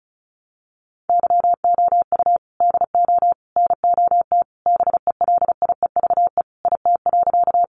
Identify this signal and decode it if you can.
Morse code, heard as a keyed tone, about 34 wpm, 700 Hz. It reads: YOU DO NOT BELIEVE IT.
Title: Sentences from Top 1000 Words 35wpm